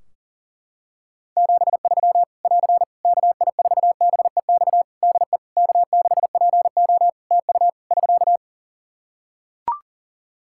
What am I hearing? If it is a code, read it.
Morse code, heard as a keyed tone, about 40 wpm, 700 Hz. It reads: 73 <AR> KI4BEX DE K6PO TU <SK>
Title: Practice QSO - Exchange 4 - Ending - 40wpm